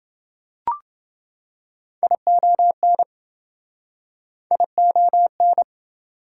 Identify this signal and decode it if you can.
Morse code, heard as a keyed tone, about 28 wpm, 700 Hz. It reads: ION ION